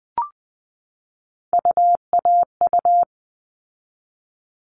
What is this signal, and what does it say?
Morse code, heard as a keyed tone, about 20 wpm, 700 Hz. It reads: UAU
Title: All Cut Numbers - 3-Digit 20wpm